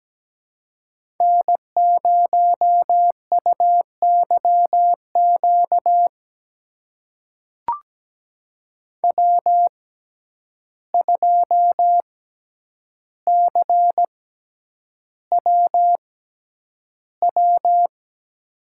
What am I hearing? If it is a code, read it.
Morse code, heard as a keyed tone, about 17 wpm, spaced as for 9 wpm, 700 Hz. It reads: N0UYQ W2CWW
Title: USA Call Signs - 5 Characters - 6x Character Spacing - 17wpm